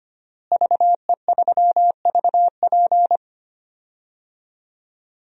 Morse code, heard as a keyed tone, about 25 wpm, 700 Hz. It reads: VE3VP
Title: DXCC-100 Call Signs - 5 Characters - Repeated 2x - 25wpm